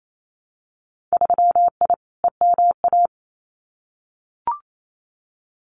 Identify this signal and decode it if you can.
Morse code, heard as a keyed tone, about 28 wpm, 700 Hz. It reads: 3I EMA
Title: Winter Field Day Exchanges Encore 28wpm